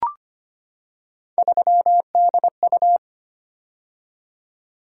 Morse code, heard as a keyed tone, about 25 wpm, 700 Hz. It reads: 3DU
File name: Three Letter-Number Combinations - 25wpm